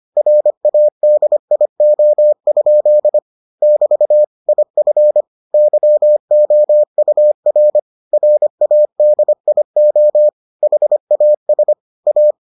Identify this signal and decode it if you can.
Morse code, heard as a keyed tone, about 25 wpm, 600 Hz. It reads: RADIO? <BT> IF YOUR RADIO HAS A